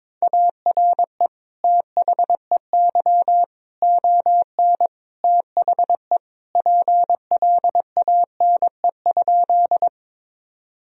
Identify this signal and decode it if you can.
Morse code, heard as a keyed tone, about 22 wpm, 700 Hz. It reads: ARE THEY ON THE PLANE?
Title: Sentences from Top 1000 Words - Review Part 3 - 22wpm